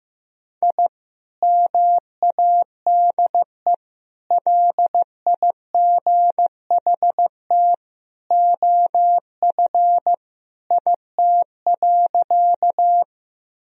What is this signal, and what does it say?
Morse code, heard as a keyed tone, about 15 wpm, 700 Hz. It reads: I MADE LIGHT OF IT.
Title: Sentences from Top 1000 Words - Review Part 1 - 15wpm